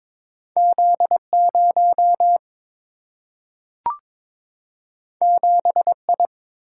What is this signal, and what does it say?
Morse code, heard as a keyed tone, about 22 wpm, 700 Hz. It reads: Z 0 7 I
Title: Single-Letter Single-Number 22wpm